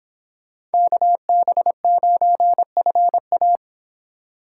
Morse code, heard as a keyed tone, about 31 wpm, 700 Hz. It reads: KB9FA